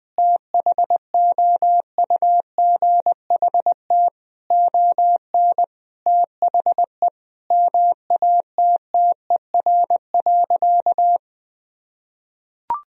Morse code, heard as a keyed tone, about 20 wpm, 700 Hz. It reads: THOUGHT ON THE MATTER.
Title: Sentences from Top 1000 Words - Review Part 1 - 20wpm